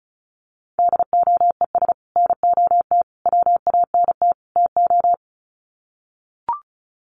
Morse code, heard as a keyed tone, about 34 wpm, 700 Hz. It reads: DOES NOT WANT TO
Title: Sets of 4 Words 35wpm